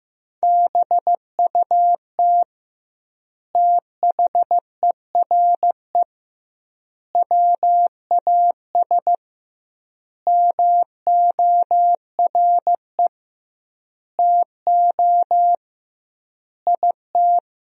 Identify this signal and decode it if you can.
Morse code, heard as a keyed tone, about 15 wpm, 700 Hz. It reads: BUT THERE WAS MORE TO IT
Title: 2x Word-Spacing - Sentences from Top 100 Words - 15wpm